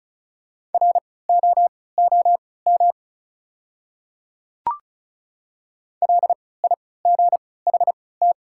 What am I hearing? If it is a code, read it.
Morse code, heard as a keyed tone, about 36 wpm, spaced as for 20 wpm, 700 Hz. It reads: ROOM LIGHT
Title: 3x Character Spacing - Top 300 Words - 35wpm